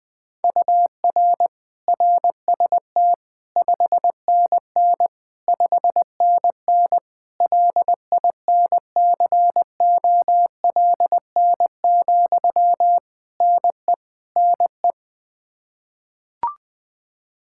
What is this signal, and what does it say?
Morse code, heard as a keyed tone, about 20 wpm, 700 Hz. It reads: UR RST 5NN 5NN LINCOLN, NE NE
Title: Practice QSO - Exchange 1 - Beginning - 20wpm